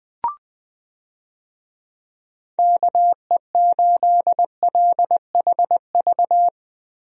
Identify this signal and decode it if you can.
Morse code, heard as a keyed tone, about 20 wpm, 700 Hz. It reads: KE8LHV
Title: USA Call Signs - 6 Characters - Repeated 3x - 20wpm